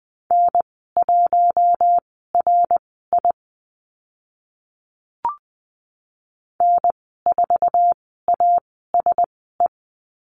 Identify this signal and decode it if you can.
Morse code, heard as a keyed tone, about 20 wpm, spaced as for 15 wpm, 700 Hz. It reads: N1RI N4ASE
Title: 055 Lesson 10g - Letter r - Call Signs 20wpm